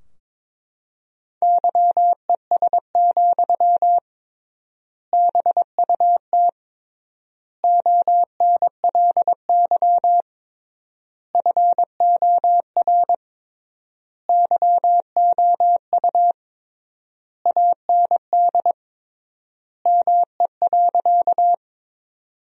Morse code, heard as a keyed tone, about 22 wpm, 700 Hz. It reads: YES, BUT ONLY FOR YOU AND ME.
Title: 3x Word-Spacing - Sentences from Top 500 Words - 22wpm